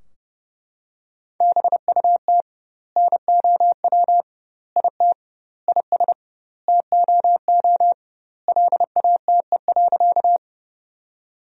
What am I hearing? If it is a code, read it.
Morse code, heard as a keyed tone, about 32 wpm, 700 Hz. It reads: BUT NOW IT IS TOO LATE.
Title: 2x Word-Spacing - Sentences from Top 300 Words - 30wpm